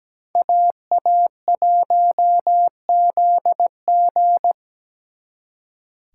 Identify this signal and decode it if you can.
Morse code, heard as a keyed tone, about 20 wpm, 700 Hz. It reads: AA1ZG
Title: Speed Racing - USA 2x2 Call Signs 17wpm